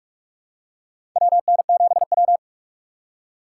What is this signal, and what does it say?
Morse code, heard as a keyed tone, about 46 wpm, 690 Hz. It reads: WN7W
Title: USA Call Signs - 4 Characters - 8x Character Spacing - 45wpm